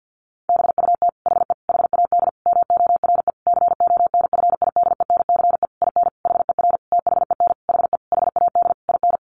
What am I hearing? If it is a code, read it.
Morse code, heard as a keyed tone, about 50 wpm, 700 Hz. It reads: BUT HE HAD MORE CONFIDENCE IN HER THEN HE HAD IN
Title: Sentences from Top 1500 Words 50wpm